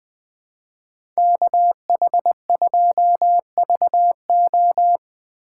Morse code, heard as a keyed tone, about 20 wpm, 700 Hz. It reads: KH2VO
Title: USA Call Signs - 5 Characters - 3x Character Spacing - 20wpm